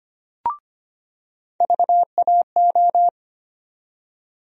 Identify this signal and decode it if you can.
Morse code, heard as a keyed tone, about 25 wpm, 700 Hz. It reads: VAO
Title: USA Call Sign Postfixes - 3 Letters 25wpm